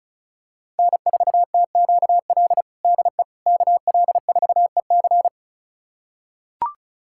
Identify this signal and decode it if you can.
Morse code, heard as a keyed tone, about 36 wpm, 700 Hz. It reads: N4TQL DE KL4EC
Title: USA Callsign DE Callsign - 5 Characters - 35wpm